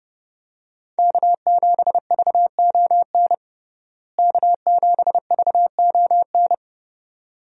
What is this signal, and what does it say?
Morse code, heard as a keyed tone, about 30 wpm, 700 Hz. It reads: K7VON K7VON
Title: USA Call Signs - 5 Characters - Repeated 2x - 30wpm